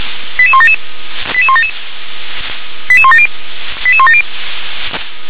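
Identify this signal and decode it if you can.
Five-tone selective call (ZVEI), heard as quick runs of five tones, tones of about 70 ms. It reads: 88177 88177 88170 88170
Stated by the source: ZVEI five-tone selective call received with an SDR.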